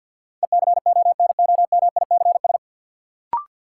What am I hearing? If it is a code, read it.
Morse code, heard as a keyed tone, about 50 wpm, 700 Hz. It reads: ECONOMICS
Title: Sentences from Top 1000 Words - Review Part 3 - New Word Drill - 50wpm